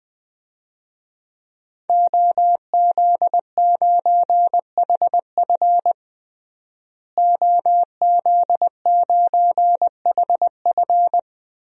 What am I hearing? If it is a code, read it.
Morse code, heard as a keyed tone, about 20 wpm, 700 Hz. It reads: OZ9HF OZ9HF
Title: DXCC-100 Call Signs - 5 Characters - Repeated 3x - 20wpm